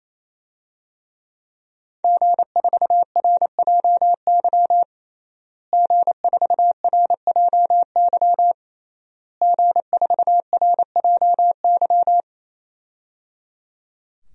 Morse code, heard as a keyed tone, about 28 wpm, 700 Hz. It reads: G4RJY G4RJY G4RJY
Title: DXCC-100 Call Signs - 5 Characters - Repeated 3x - 28wpm